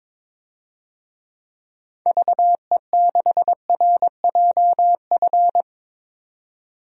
Morse code, heard as a keyed tone, about 22 wpm, 700 Hz. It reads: VE6RJF
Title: DXCC-100 Call Signs - 6 Characters - 22wpm